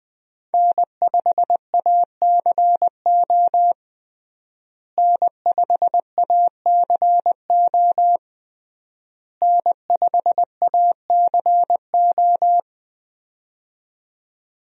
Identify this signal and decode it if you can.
Morse code, heard as a keyed tone, about 20 wpm, 700 Hz. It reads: N5ACO N5ACO N5ACO
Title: USA Call Signs - 5 Characters - Repeated 3x - 20wpm